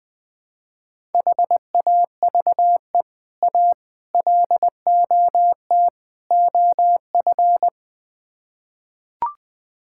Morse code, heard as a keyed tone, about 20 wpm, 700 Hz. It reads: HAVE A LOT OF
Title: Build A Phrase - 4-Word Phrases - 20wpm